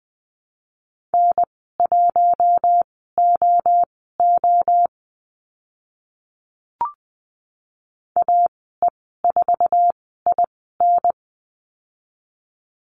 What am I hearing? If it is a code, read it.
Morse code, heard as a keyed tone, about 20 wpm, spaced as for 15 wpm, 700 Hz. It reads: N1OO AE4IN
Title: 048 Lesson 9g - Number 4 - Call Signs 20wpm